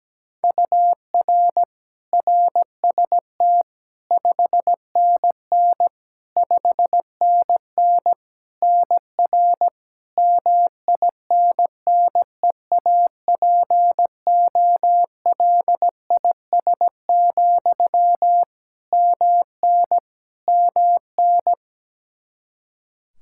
Morse code, heard as a keyed tone, about 17 wpm, 700 Hz. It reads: UR RST 5NN 5NN NR MINNEAPOLIS, MN MN
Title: Practice QSO - Exchange 1 - Beginning - 17wpm